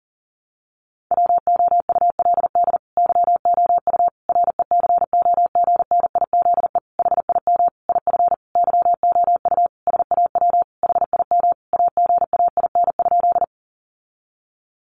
Morse code, heard as a keyed tone, about 40 wpm, 700 Hz. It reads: WOULD YOU RECOGNIZE HIM IF YOU SAW HIM AGAIN?
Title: Sentences from Top 1400 Words 40wpm